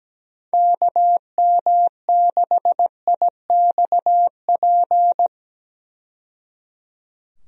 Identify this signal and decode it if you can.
Morse code, heard as a keyed tone, about 17 wpm, 700 Hz. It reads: KM6IXP
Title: USA Call Signs - 6 Characters - 17wpm